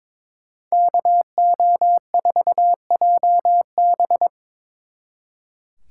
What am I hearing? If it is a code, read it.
Morse code, heard as a keyed tone, about 26 wpm, 700 Hz. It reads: KO4JB
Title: Speed Racing - USA 2x2 Call Signs 22wpm